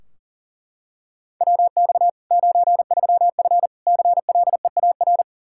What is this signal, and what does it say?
Morse code, heard as a keyed tone, about 40 wpm, 700 Hz. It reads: WX 93F CLEAR
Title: Practice QSO - Exchange 2 - Info - 40wpm